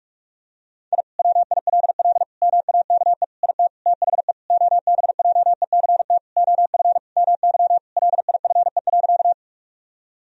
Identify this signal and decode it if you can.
Morse code, heard as a keyed tone, about 46 wpm, 700 Hz. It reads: I WILL MAKE IT THE OBJECT OF MY LIFE.
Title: Sentences from Top 1000 Words - Review Part 3 - 45wpm